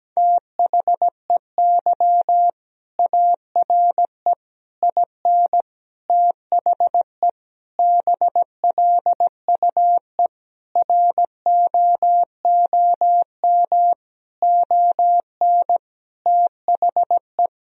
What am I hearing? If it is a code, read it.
Morse code, heard as a keyed tone, about 17 wpm, 700 Hz. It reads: THEY ARE IN THE BLUE ROOM ON THE
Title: Sentences from Top 1000 Words - Review Part 3 - 17wpm